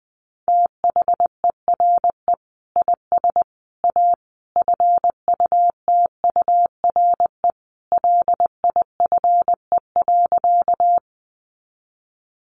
Morse code, heard as a keyed tone, about 20 wpm, 700 Hz. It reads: THERE IS A FUTURE LIFE.
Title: Sentences from Top 700 Words 20wpm